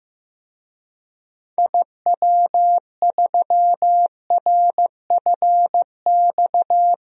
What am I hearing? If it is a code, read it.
Morse code, heard as a keyed tone, about 15 wpm, 700 Hz. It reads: IW3RFX
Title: DXCC-100 Call Signs - 6 Characters - Repeated 2x - 15wpm